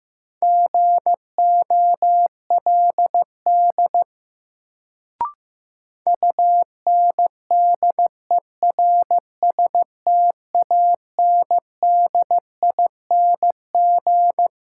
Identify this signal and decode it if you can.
Morse code, heard as a keyed tone, about 16 wpm, 700 Hz. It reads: GOLD UNDERSTANDING
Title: Sentences from Top 1000 Words - Review Part 3 - New Word Drill - 15wpm